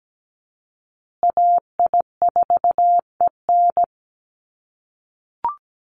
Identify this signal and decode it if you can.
Morse code, heard as a keyed tone, about 17 wpm, 700 Hz. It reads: AI4EN
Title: USA Call Signs Repeated 2x 17wpm